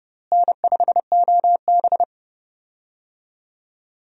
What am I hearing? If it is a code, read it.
Morse code, heard as a keyed tone, about 29 wpm, 700 Hz. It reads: N5OB
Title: USA Call Signs - 4 Characters - Repeated 2x - 30wpm